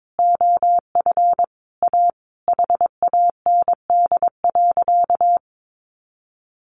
Morse code, heard as a keyed tone, about 22 wpm, 700 Hz. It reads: OF A HAND.
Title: A Sound Of Thunder 22wpm